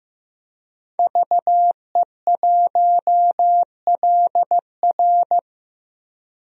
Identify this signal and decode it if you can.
Morse code, heard as a keyed tone, about 15 wpm, 700 Hz. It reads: VE1LR
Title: DXCC-100 Call Signs - 5 Characters - Repeated 2x - 15wpm